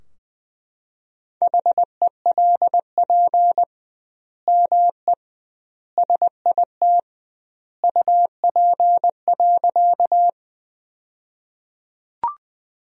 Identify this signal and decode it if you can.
Morse code, heard as a keyed tone, about 20 wpm, 700 Hz. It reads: HELP ME SIT UP.
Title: 2x Word-Spacing - Sentences from Top 500 Words - 20wpm